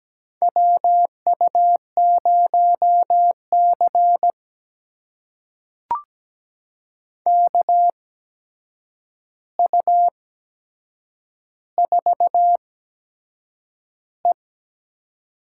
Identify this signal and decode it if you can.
Morse code, heard as a keyed tone, about 17 wpm, spaced as for 7 wpm, 700 Hz. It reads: WU0C KU4E
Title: USA Call Signs - 8x Character Spacing - 17wpm